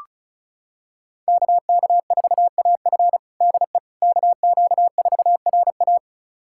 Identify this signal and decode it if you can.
Morse code, heard as a keyed tone, about 34 wpm, 700 Hz. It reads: KK4AF DE KQ4RA
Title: USA Callsign DE Callsign - 5 Characters - 35wpm